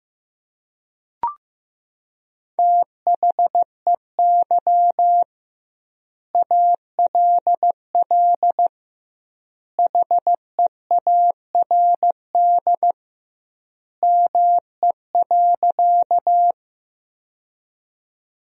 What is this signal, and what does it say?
Morse code, heard as a keyed tone, about 15 wpm, 700 Hz. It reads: THEY ALL HEARD ME.
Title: 2x Word-Spacing - Sentences from Top 400 Words - 15wpm